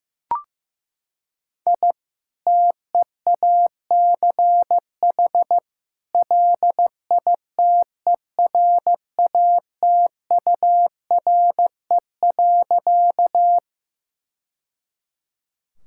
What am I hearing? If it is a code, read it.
Morse code, heard as a keyed tone, about 15 wpm, 700 Hz. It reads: I TEACH LITERATURE.
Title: Sentences from Top 1000 Words - Review Part 6 - 15wpm